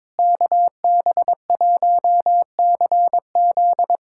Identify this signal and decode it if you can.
Morse code, heard as a keyed tone, about 22 wpm, 700 Hz. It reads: KB1CZ
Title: USA Callsign DE Callsign - 5 Characters - 22wpm